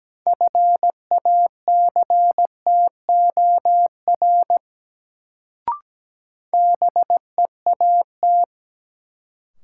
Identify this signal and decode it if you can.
Morse code, heard as a keyed tone, about 17 wpm, 700 Hz. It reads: FACTOR BEAT
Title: Sentences from Top 1300 Words - New Word Drill Alt - 17wpm